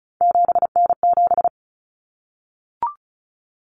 Morse code, heard as a keyed tone, about 34 wpm, 700 Hz. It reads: 7N7
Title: Common Cut Numbers - 3-Digit 35wpm